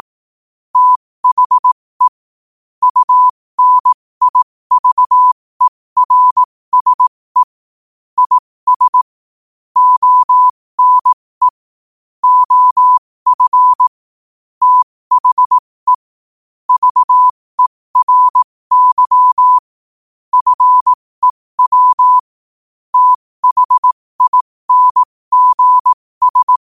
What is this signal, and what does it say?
Morse code, heard as a keyed tone, about 18 wpm, spaced as for 15 wpm, 1000 Hz. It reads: THE UNIVERSE IS ONE OF THE VERY FEW THINGS
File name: qotdfiles / 15wpm